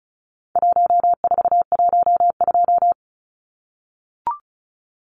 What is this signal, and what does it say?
Morse code, heard as a keyed tone, about 34 wpm, 700 Hz. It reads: 1412
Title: Numbers - 4-Digit 35wpm